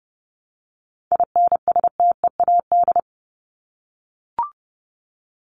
Morse code, heard as a keyed tone, about 30 wpm, 700 Hz. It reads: INSTEAD
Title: Top 501-750 Words 30wpm